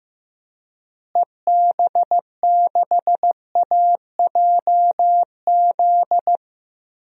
Morse code, heard as a keyed tone, about 15 wpm, 700 Hz. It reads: EB6AJZ
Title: DXCC-100 Call Signs - 6 Characters - Repeated 3x - 15wpm